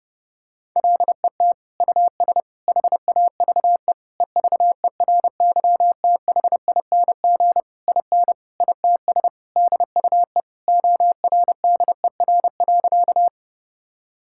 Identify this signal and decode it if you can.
Morse code, heard as a keyed tone, about 32 wpm, 700 Hz. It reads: LET US HAVE EVERYTHING IN ITS DUE ORDER.
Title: Sentences from Top 1000 Words - Review Part 6 - 30wpm